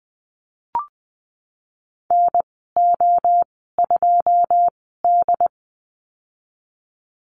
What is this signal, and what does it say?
Morse code, heard as a keyed tone, about 20 wpm, spaced as for 15 wpm, 700 Hz. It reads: NO2D
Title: 167 Lesson 27g - Number 7 - Call Signs 20wpm